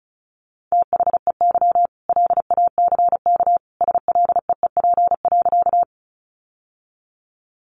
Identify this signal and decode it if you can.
Morse code, heard as a keyed tone, about 34 wpm, 700 Hz. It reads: THEY LACK SLEEP.
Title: Sentences from Top 1100 Words 35wpm